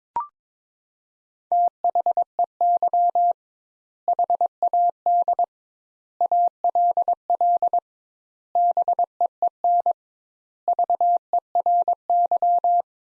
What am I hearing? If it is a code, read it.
Morse code, heard as a keyed tone, about 22 wpm, 700 Hz. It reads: THEY HAD ALL BEEN VERY
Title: Sentences Top 200 Words - 22wpm